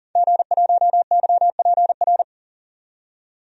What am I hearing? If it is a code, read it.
Morse code, heard as a keyed tone, about 40 wpm, 700 Hz. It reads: G1YPR
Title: DXCC-100 Call Signs - 5 Characters - 40wpm